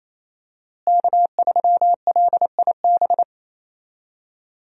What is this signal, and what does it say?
Morse code, heard as a keyed tone, about 27 wpm, 700 Hz. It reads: K3LIB
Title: USA Call Signs - 5 Characters - Repeated 3x - 28wpm